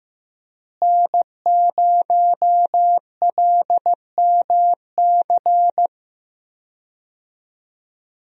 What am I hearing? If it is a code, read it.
Morse code, heard as a keyed tone, about 15 wpm, 700 Hz. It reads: N0LMC
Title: USA Call Signs - 5 Characters - Repeated 3x - 15wpm